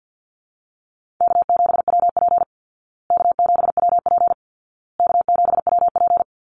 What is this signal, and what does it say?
Morse code, heard as a keyed tone, about 48 wpm, 700 Hz. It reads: K7WP K7WP K7WP
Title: Easy Call Signs Repeated 3x - 50wpm